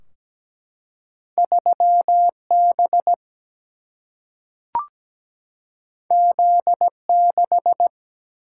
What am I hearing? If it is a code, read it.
Morse code, heard as a keyed tone, about 17 wpm, 700 Hz. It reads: 3 B Z 6
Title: Single-Letter Single-Number 17wpm